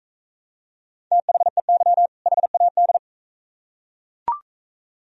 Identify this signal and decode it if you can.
Morse code, heard as a keyed tone, about 42 wpm, 690 Hz. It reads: THEY HAD